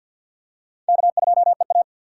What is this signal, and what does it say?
Morse code, heard as a keyed tone, about 49 wpm, 700 Hz. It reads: K2EA
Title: USA Call Signs - 4x Character Spacing - 50wpm